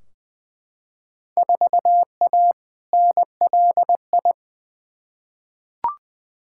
Morse code, heard as a keyed tone, about 20 wpm, 700 Hz. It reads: 4A NLI
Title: Field Day Exchanges Encore - 20wpm